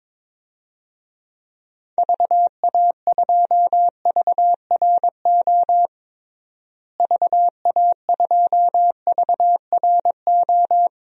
Morse code, heard as a keyed tone, about 21 wpm, 700 Hz. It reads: VA2VRO VA2VRO
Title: DXCC-100 Call Signs - 6 Characters - Repeated 2x - 22wpm